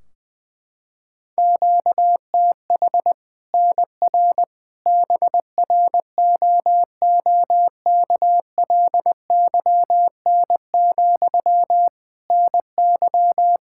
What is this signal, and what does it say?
Morse code, heard as a keyed tone, about 21 wpm, 700 Hz. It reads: QTH NR BROOKLYN, NY
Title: Practice QSO - Exchange 1 - Beginning - 20wpm